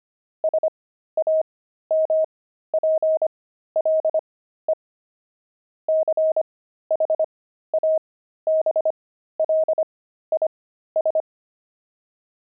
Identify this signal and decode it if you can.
Morse code, heard as a keyed tone, about 25 wpm, spaced as for 13 wpm, 630 Hz. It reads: SAMPLE CHABLIS
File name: pangram-fs-13-cs-25-wine